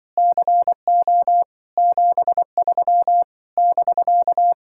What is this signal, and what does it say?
Morse code, heard as a keyed tone, about 24 wpm, 700 Hz. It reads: CO 73 <BK>
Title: Speed-Racing Common QSO Phrases Encore - 20wpm